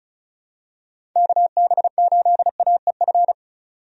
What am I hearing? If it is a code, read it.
Morse code, heard as a keyed tone, about 34 wpm, 700 Hz. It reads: KB8AEF
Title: USA Call Signs - 6 Characters - 35wpm